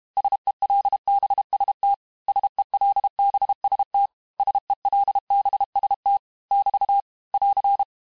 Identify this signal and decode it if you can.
Morse code, heard as a keyed tone, about 32 wpm, 800 Hz. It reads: SELBST SELBST SELBST <BT> <AR>
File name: cours-du-mercredi / audios / cours-868-teste-MOTS-vitesse-30mpm